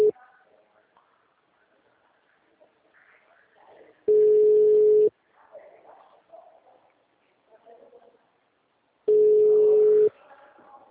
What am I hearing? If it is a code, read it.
Ringback tone (Europe, 425 Hz).